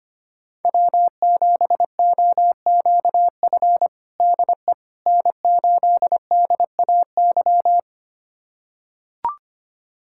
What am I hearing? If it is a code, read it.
Morse code, heard as a keyed tone, about 25 wpm, 700 Hz. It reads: W7OQF DE N8DAY
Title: USA Callsign DE Callsign - 5 Characters - 25wpm